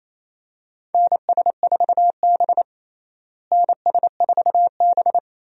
Morse code, heard as a keyed tone, about 27 wpm, 700 Hz. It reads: NS4B NS4B
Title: USA Call Signs - 4 Characters - Repeated 3x - 28wpm